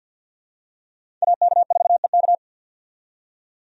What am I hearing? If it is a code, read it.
Morse code, heard as a keyed tone, about 50 wpm, 700 Hz. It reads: AK4EK
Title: USA Call Signs - 5 Characters - 50wpm